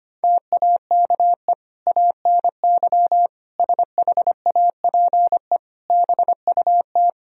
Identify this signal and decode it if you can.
Morse code, heard as a keyed tone, about 25 wpm, 700 Hz. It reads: TAKE ANY SHAPE BUT
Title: Sentences from Top 1000 Words - Review Part 3 - 25wpm